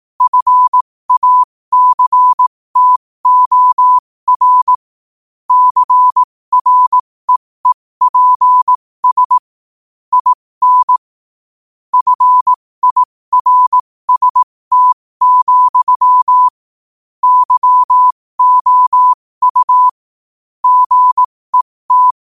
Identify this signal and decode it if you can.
Morse code, heard as a keyed tone, about 18 wpm, spaced as for 15 wpm, 1000 Hz. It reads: FACTOR CREEPS IN FIRST, YOU GET
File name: qotdfiles / 15wpm